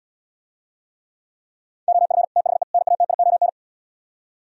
Morse code, heard as a keyed tone, about 77 wpm, 700 Hz. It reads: OF THE NATION